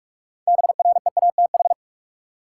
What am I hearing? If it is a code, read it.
Morse code, heard as a keyed tone, about 46 wpm, 700 Hz. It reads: BREATH
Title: Sets of 5 Words Encore - New Word Drill - 45wpm